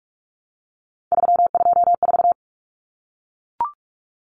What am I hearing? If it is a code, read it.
Morse code, heard as a keyed tone, about 44 wpm, 700 Hz. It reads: 324
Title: Numbers - 3-Digit 45wpm